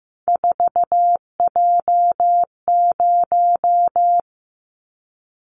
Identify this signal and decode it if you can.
Morse code, heard as a keyed tone, about 15 wpm, 700 Hz. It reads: 4J0
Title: Callsign Prefixes for Top 100 DXCC - 15wpm